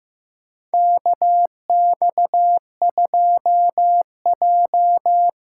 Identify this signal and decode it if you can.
Morse code, heard as a keyed tone, about 16 wpm, 700 Hz. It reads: KX2J
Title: USA Call Signs - 4 Characters - 6x Character Spacing - 15wpm